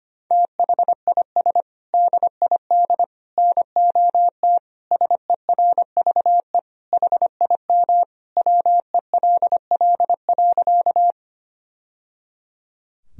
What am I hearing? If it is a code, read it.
Morse code, heard as a keyed tone, about 25 wpm, 700 Hz. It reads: THIS DID NOT SERVE HIM WELL.
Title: Sentences from Top 1000 Words - Review Part 2 - 25wpm